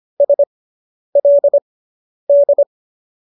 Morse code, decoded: SLD

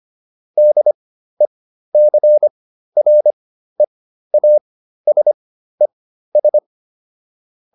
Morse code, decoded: DECREASES

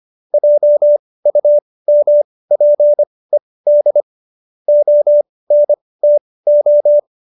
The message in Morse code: JUMPED ONTO

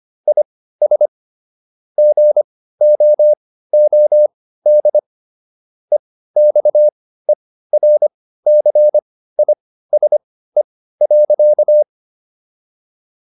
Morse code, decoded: IS GOOD EXERCISE.

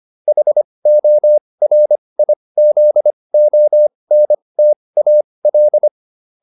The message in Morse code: HORIZONTAL

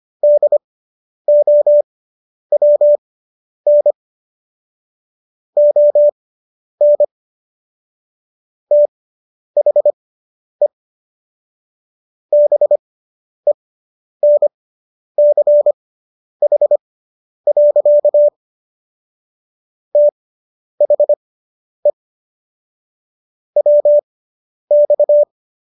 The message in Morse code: DOWN ON THE BENCH. THE WX

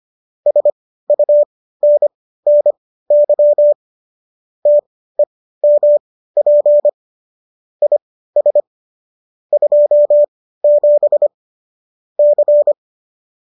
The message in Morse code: SUNNY TEMP IS 27 C

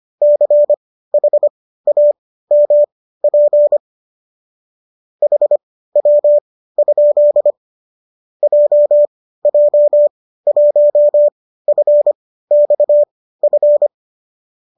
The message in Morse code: CHAMP HW? JJ1FXF

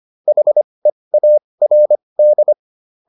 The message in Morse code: HEARD

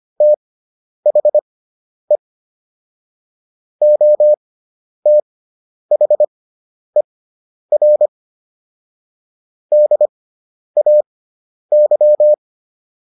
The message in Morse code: THE OTHER DAY

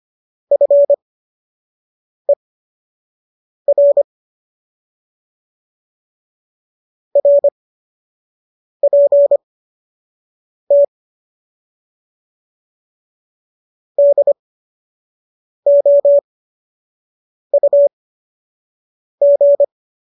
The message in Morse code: FER RPT DOUG

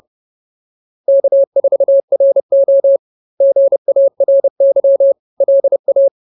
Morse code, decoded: K4RO GARY LA

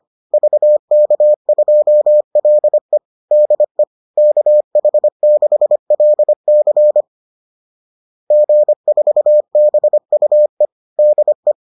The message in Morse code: VK2LE DE KH6LC G4BUE DE

